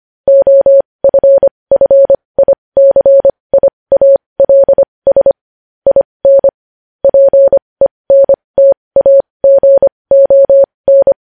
Morse code, heard as a keyed tone, about 25 wpm, spaced as for 20 wpm, 550 Hz. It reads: OFFICIALS IN PENTAGON
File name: podcast / News headlines-1x-25x20